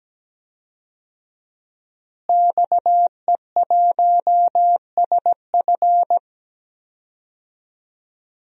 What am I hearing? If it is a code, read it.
Morse code, heard as a keyed tone, about 17 wpm, 700 Hz. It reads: XE1SF
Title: DXCC-100 Call Signs - 5 Characters - Repeated 3x - 17wpm